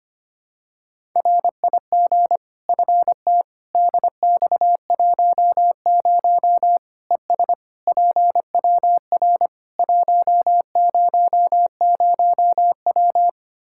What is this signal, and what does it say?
Morse code, heard as a keyed tone, about 25 wpm, 700 Hz. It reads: RIG FT DX10 ES PWR 100W
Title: Practice QSO - Exchange 2 - Info - 25wpm